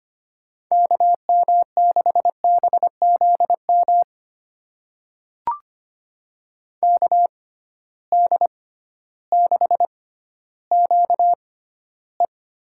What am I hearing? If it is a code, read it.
Morse code, heard as a keyed tone, about 25 wpm, spaced as for 13 wpm, 700 Hz. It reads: KM6BZM KD6QE
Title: USA Call Signs - 6 Characters - 6x Character Spacing - 25wpm